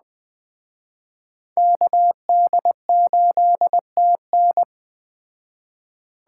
Morse code, heard as a keyed tone, about 19 wpm, 700 Hz. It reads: KD8TN